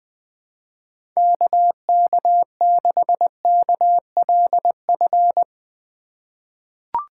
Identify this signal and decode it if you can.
Morse code, heard as a keyed tone, about 20 wpm, 700 Hz. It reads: KK6KLF
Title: USA Call Signs - 6 Characters - 4x Character Spacing - 20wpm